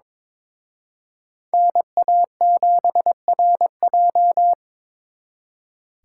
Morse code, heard as a keyed tone, about 27 wpm, 700 Hz. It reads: NA7RJ
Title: Speed Racing - USA 2x2 Call Signs 22wpm